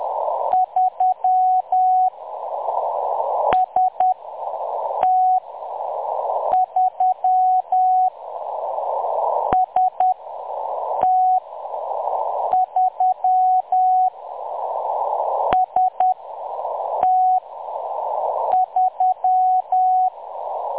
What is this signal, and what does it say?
Morse code, heard as a keyed tone, about 10 wpm, 740 Hz. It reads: 3 S T 3 S T 3 S T 3
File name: M23-ST3- -0858utc-10184khz